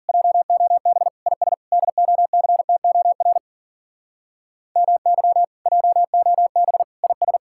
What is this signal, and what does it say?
Morse code, heard as a keyed tone, about 51 wpm, 700 Hz. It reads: JOB IS DOCTOR MY JOB IS